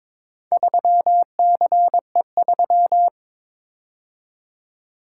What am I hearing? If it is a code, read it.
Morse code, heard as a keyed tone, about 22 wpm, 700 Hz. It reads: 3CE3
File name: practice / 22wpm / Four Letter-Number Combinations - 22wpm